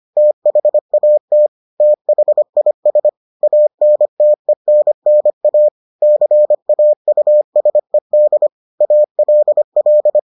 Morse code, heard as a keyed tone, about 25 wpm, 600 Hz. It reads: THAT THIS ANTENNA CAUSED ALL